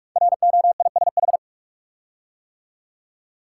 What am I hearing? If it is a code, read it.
Morse code, heard as a keyed tone, about 44 wpm, 700 Hz. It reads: ROISH
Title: Code Groups - 5 Letters - 45wpm